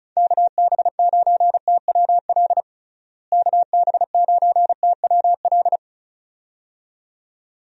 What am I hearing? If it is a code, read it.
Morse code, heard as a keyed tone, about 36 wpm, 700 Hz. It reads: KB9TWL KB9TWL
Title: USA Call Signs - 6 Characters - Repeated 2x - 35wpm